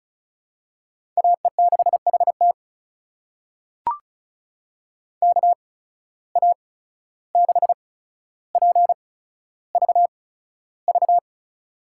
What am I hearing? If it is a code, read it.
Morse code, heard as a keyed tone, about 36 wpm, spaced as for 14 wpm, 700 Hz. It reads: AE6HT KA6PVV
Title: USA Call Signs - 8x Character Spacing - 35wpm